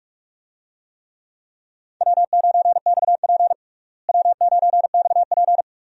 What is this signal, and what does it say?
Morse code, heard as a keyed tone, about 41 wpm, 700 Hz. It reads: W9XP W9XP
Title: USA Call Signs - 4 Characters - Repeated 3x - 45wpm